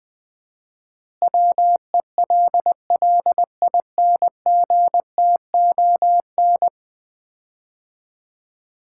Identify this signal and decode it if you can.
Morse code, heard as a keyed tone, about 20 wpm, 700 Hz. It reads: WELLINGTON
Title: Common QSO Phrases - 4x Word-Spacing - 20wpm